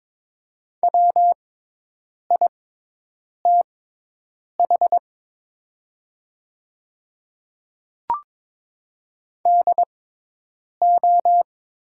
Morse code, heard as a keyed tone, about 22 wpm, spaced as for 8 wpm, 690 Hz. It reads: WITH DO
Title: 6x Character Spacing - Top 100 Words - 22wpm